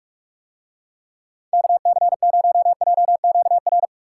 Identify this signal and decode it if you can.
Morse code, heard as a keyed tone, about 44 wpm, 700 Hz. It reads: KC0JQR